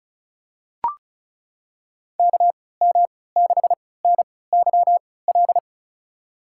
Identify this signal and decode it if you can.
Morse code, heard as a keyed tone, about 36 wpm, spaced as for 20 wpm, 700 Hz. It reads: KM6NYL
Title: USA Call Signs - 6 Characters - 3x Character Spacing - 35wpm